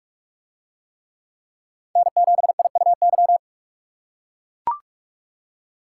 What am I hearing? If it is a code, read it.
Morse code, heard as a keyed tone, about 42 wpm, 700 Hz. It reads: N7IUY